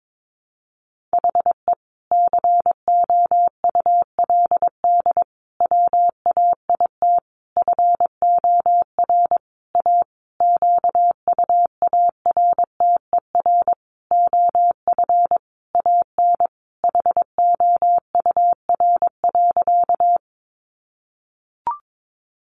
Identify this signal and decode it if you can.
Morse code, heard as a keyed tone, about 22 wpm, 700 Hz. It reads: HE COULD WAIT FOR A QUARTER OF AN HOUR.
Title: Sentences from Top 1300 Words 22wpm